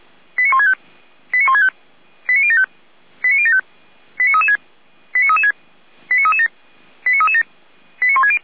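Five-tone selective call (ZVEI), heard as quick runs of five tones, tones of about 70 ms. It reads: 89165 89165 89075 89075 89336 89336 89337 89337 89160
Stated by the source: Received via SDR as a ZVEI five-tone call, address 89078.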